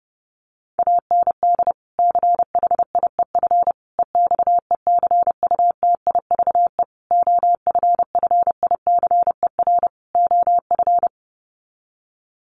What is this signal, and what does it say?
Morse code, heard as a keyed tone, about 30 wpm, 700 Hz. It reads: AND CHIEF EXECUTIVE OFFICER OF
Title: Sets of 5 Words Encore2 30wpm